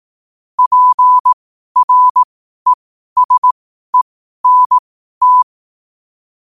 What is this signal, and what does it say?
Morse code, heard as a keyed tone, about 18 wpm, spaced as for 12 wpm, 1000 Hz. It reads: PRESENT
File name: qotdfiles / 12wpm